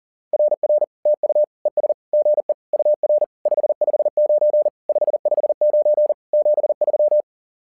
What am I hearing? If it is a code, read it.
Morse code, heard as a keyed tone, about 40 wpm, 600 Hz. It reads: RR TU ES GE UR 559 559 73
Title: SOTA QSO Practice 40wpm